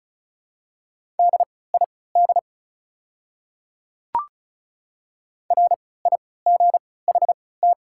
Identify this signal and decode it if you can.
Morse code, heard as a keyed tone, about 34 wpm, spaced as for 20 wpm, 700 Hz. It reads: DID RIGHT
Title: 3x Character Spacing - Top 300 Words - 35wpm